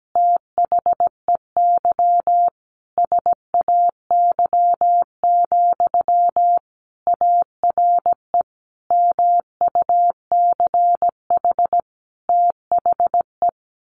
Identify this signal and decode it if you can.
Morse code, heard as a keyed tone, about 17 wpm, 700 Hz. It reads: THEY SAY, ARE MUCH THE
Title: Sentences from Top 1100 Words 17wpm